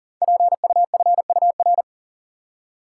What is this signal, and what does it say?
Morse code, heard as a keyed tone, about 39 wpm, 700 Hz. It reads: PUFUR
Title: Code Groups - 5 Letters - 40wpm